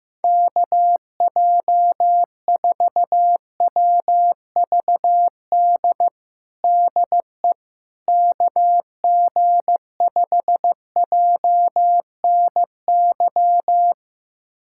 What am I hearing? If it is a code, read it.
Morse code, heard as a keyed tone, about 15 wpm, 700 Hz. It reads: KJ4WVD DE KG5JNY
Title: USA Callsign DE Callsign - 6 Characters - 15wpm